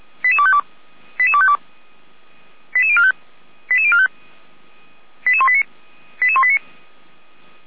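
Five-tone selective call (ZVEI), heard as quick runs of five tones, tones of about 80 ms. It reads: 88252 88252 88045 88045 88189 88189
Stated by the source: ZVEI five-tone selective call received on SDR, from feuerwehr-orlamuende (fire brigade Orlamünde) server.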